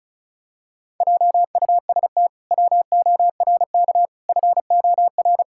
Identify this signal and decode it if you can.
Morse code, heard as a keyed tone, about 36 wpm, 700 Hz. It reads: JUST WORK FOR
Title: Pangram Challenge v2 - 35wpm